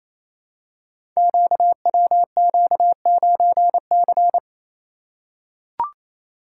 Morse code, heard as a keyed tone, about 28 wpm, 700 Hz. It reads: QWQ9C
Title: Code Groups - 5 Letter:Numbers - 28wpm